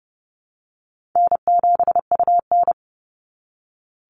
Morse code, heard as a keyed tone, about 30 wpm, 700 Hz. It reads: N7UN
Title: NA SOTA Activator Callsigns 30wpm